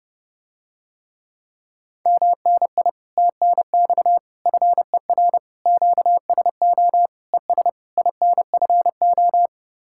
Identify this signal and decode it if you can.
Morse code, heard as a keyed tone, about 30 wpm, 700 Hz. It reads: MNI TNX FER QSO ES INFO